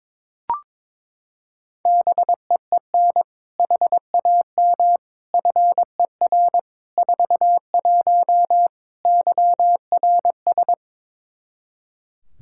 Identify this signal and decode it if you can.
Morse code, heard as a keyed tone, about 22 wpm, 700 Hz. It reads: BEEN HAM FER 41 YRS
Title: K1USN SST QSO Practice 22wpm